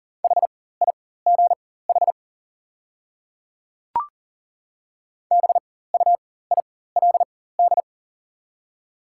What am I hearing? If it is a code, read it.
Morse code, heard as a keyed tone, about 40 wpm, spaced as for 19 wpm, 690 Hz. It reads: HIGH BUILD
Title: 4x Character Spacing - Top 300 Words - 40wpm